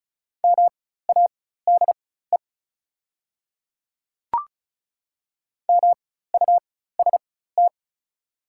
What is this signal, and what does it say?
Morse code, heard as a keyed tone, about 34 wpm, spaced as for 16 wpm, 700 Hz. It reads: MADE MUST
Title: 4x Character Spacing - Top 300 Words - 35wpm